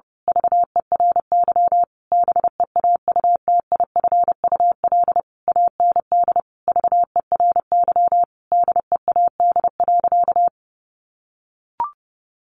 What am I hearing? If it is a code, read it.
Morse code, heard as a keyed tone, about 30 wpm, 700 Hz. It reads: VERY BEAUTIFUL AND VERY DEAD.
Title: A Sound Of Thunder 30wpm